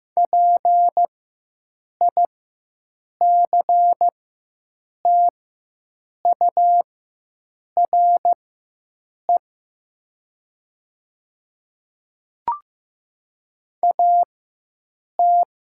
Morse code, heard as a keyed tone, about 15 wpm, spaced as for 7 wpm, 700 Hz. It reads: PICTURE AT
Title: 4x Character Spacing - Top 200 Words - 15wpm